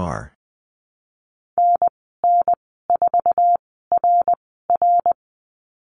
Morse code, heard as a keyed tone, about 19 wpm, spaced as for 14 wpm, 700 Hz. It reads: NN4RR
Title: 062 Lesson 11g - Letter h - Call Signs 20wpm